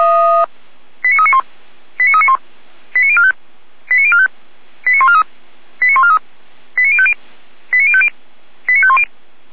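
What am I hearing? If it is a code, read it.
Five-tone selective call (ZVEI), heard as quick runs of five tones, tones of about 70 ms. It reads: 89391 89391 89045 89045 89143 89143 89050 89050 89510
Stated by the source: ZVEI five-tone call 89086 received by SDR, from the Feuerwehr Orlamünde audio server.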